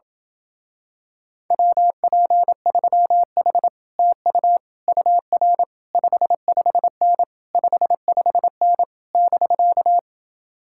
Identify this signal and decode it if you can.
Morse code, heard as a keyed tone, about 27 wpm, 700 Hz. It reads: WP3H TU UR 55N 55N <BK>